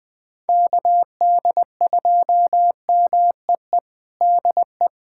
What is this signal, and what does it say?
Morse code, heard as a keyed tone, about 21 wpm, 700 Hz. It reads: KD2MEE DE
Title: USA Callsign DE Callsign - 6 Characters - 20wpm